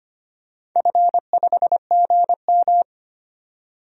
Morse code, heard as a keyed tone, about 25 wpm, 700 Hz. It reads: F5GM
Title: DXCC-100 Call Signs - 4 Characters - 25wpm